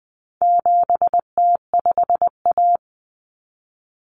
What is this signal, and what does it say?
Morse code, heard as a keyed tone, about 20 wpm, 700 Hz. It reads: 7T5A
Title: Common Cut Numbers - 4-Digit 20wpm